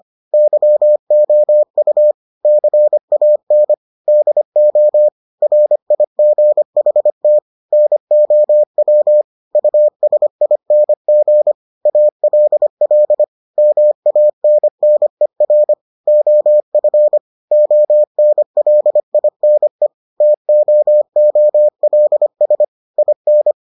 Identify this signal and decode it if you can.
Morse code, heard as a keyed tone, about 25 wpm, 600 Hz. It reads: YOU CAN DO RIGHT NOW USING ALL MANNER OF ONLINE TOOLS IN